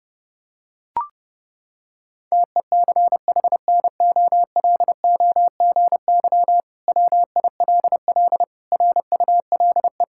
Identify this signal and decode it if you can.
Morse code, heard as a keyed tone, about 32 wpm, 700 Hz. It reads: TECHNOLOGY WILL RULE
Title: Sentences from Top 1000 Words - Review Part 5 - 30wpm